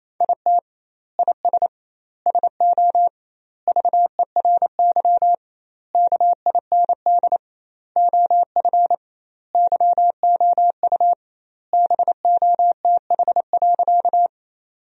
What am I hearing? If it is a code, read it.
Morse code, heard as a keyed tone, about 28 wpm, 700 Hz. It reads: IT IS SO VERY KIND OF YOU BOTH.
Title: 2x Word-Spacing - Sentences from Top 300 Words - 28wpm